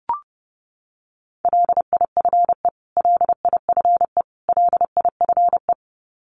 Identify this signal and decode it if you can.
Morse code, heard as a keyed tone, about 30 wpm, 700 Hz. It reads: LIFE LIFE LIFE
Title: Top 500 Words Repeated 30wpm